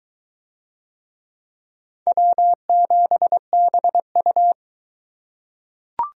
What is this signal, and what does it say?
Morse code, heard as a keyed tone, about 18 wpm, 700 Hz. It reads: W7BU
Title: Speed Racing - USA 1x2 Call Signs 15wpm